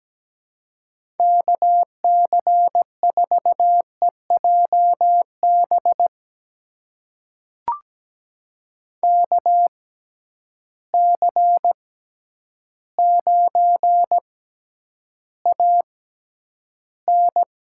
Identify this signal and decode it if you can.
Morse code, heard as a keyed tone, about 17 wpm, spaced as for 9 wpm, 700 Hz. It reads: KC4EJB KC9AN
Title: USA Call Signs - 6x Character Spacing - 17wpm